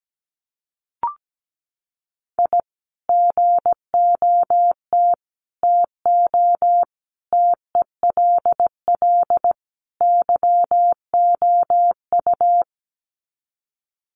Morse code, heard as a keyed tone, about 17 wpm, 700 Hz. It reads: I GOT TO TELL YOU
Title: Sets of 5 Words Encore2 17wpm